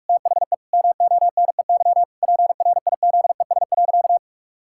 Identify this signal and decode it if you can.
Morse code, heard as a keyed tone, about 46 wpm, 700 Hz. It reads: THE MONEY PRIZES.
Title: Pangram Challenge v2 - 45wpm